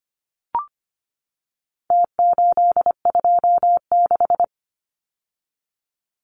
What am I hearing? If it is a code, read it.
Morse code, heard as a keyed tone, about 25 wpm, 700 Hz. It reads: T826
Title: Numbers - Contest Style - 4-Digit 25wpm